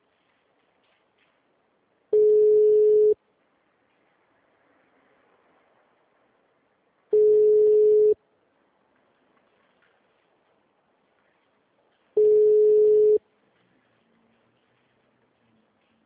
Ringback tone (Europe, 425 Hz).
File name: out- -2016- - - .680